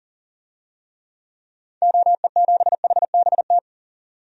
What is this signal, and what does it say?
Morse code, heard as a keyed tone, about 40 wpm, 700 Hz. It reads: OE7HBT
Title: DXCC-100 Call Signs - 6 Characters - 40wpm